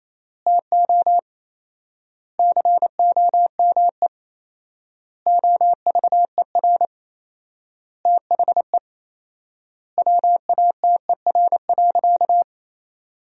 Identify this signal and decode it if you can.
Morse code, heard as a keyed tone, about 28 wpm, 700 Hz. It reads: TO COME OVER THE WATER.
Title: 4x Word-Spacing - Sentences from Top 300 Words - 28wpm